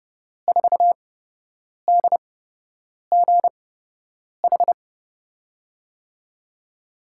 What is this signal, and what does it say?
Morse code, heard as a keyed tone, about 30 wpm, spaced as for 8 wpm, 700 Hz. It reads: 4DGH
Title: USA Call Signs - 6 Characters - 8x Character Spacing - 30wpm